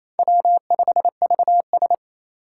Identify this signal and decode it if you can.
Morse code, heard as a keyed tone, about 28 wpm, 700 Hz. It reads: W5VS
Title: USA Call Signs - 3x Character Spacing - 28wpm